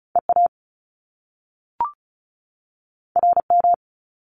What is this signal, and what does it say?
Morse code, heard as a keyed tone, about 36 wpm, 700 Hz. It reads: EA RM
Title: Top Two Letter Combinations in Words 35wpm